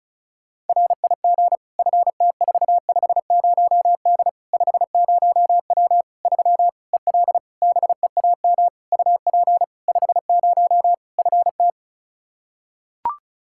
Morse code, heard as a keyed tone, about 36 wpm, 700 Hz. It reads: RIG FT450D 50W 3 EL BEAM UP 50 FT
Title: K1USN SST QSO Practice 35wpm